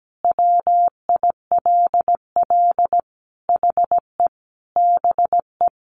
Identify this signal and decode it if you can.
Morse code, heard as a keyed tone, about 17 wpm, 700 Hz. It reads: WILL HE BE
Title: Sentences from Top 400 Words 17wpm